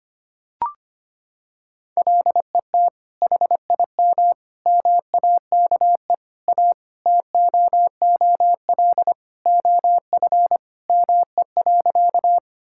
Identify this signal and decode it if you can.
Morse code, heard as a keyed tone, about 25 wpm, 700 Hz. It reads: LET HIM MAKE A TOOL OF ME.
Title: Sentences from Top 1000 Words - Review Part 6 - 25wpm